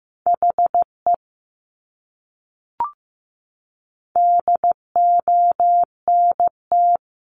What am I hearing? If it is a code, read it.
Morse code, heard as a keyed tone, about 15 wpm, 700 Hz. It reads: HE DONT
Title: Top 500 Words Repeated 15wpm